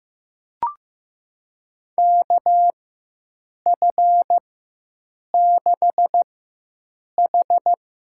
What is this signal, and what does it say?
Morse code, heard as a keyed tone, about 15 wpm, spaced as for 7 wpm, 700 Hz. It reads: KF6H
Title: USA Call Signs - 6 Characters - 4x Character Spacing - 15wpm